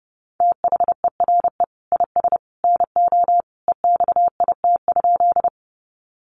Morse code, heard as a keyed tone, about 30 wpm, 700 Hz. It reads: THERE IS NO EXIT?
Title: Sentences from Top 1100 Words 30wpm